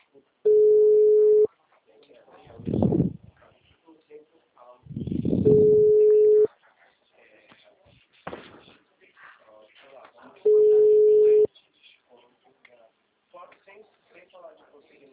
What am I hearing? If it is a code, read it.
Ringback tone (Europe, 425 Hz), with speech.